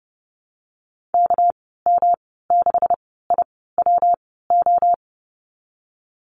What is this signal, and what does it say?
Morse code, heard as a keyed tone, about 32 wpm, spaced as for 17 wpm, 700 Hz. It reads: KM6IWO
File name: Lesson 37g - Letter x - Call Signs 30wpm